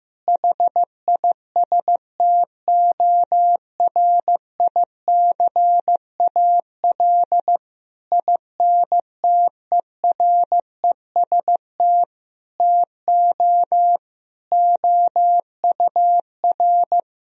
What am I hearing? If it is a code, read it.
Morse code, heard as a keyed tone, about 16 wpm, 700 Hz. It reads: HISTORICAL INTEREST TO OUR
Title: Sentences from Top 1000 Words - Review Part 4 - 15wpm